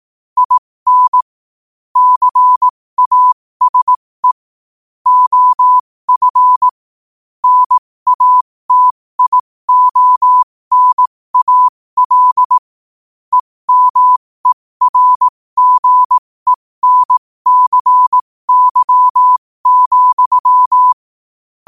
Morse code, heard as a keyed tone, about 18 wpm, spaced as for 15 wpm, 1000 Hz. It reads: IN CASE OF NATIONAL EMERGENCY,
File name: 15wpm